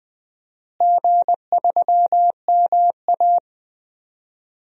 Morse code, heard as a keyed tone, about 20 wpm, 700 Hz. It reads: G3MA
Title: DXCC-100 Call Signs - 4 Characters - Repeated 3x - 20wpm